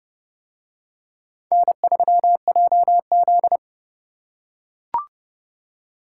Morse code, heard as a keyed tone, about 28 wpm, 700 Hz. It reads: N3JZ